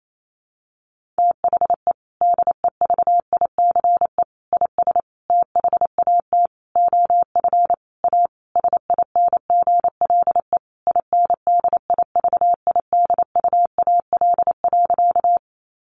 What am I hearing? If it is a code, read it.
Morse code, heard as a keyed tone, about 28 wpm, 700 Hz. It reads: THE DEVICE IS THAT OF A SINGLE INDIVIDUAL.
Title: Sentences from Top 1500 Words 28wpm